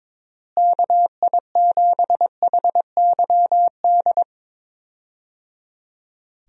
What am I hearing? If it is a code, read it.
Morse code, heard as a keyed tone, about 22 wpm, 700 Hz. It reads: KI7HYD
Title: USA Call Signs - 6 Characters - Repeated 3x - 22wpm